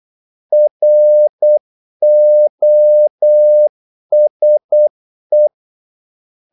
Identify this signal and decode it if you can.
Morse code, heard as a keyed tone, about 8 wpm, 600 Hz. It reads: ROSE
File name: Korte ord DK 8wpm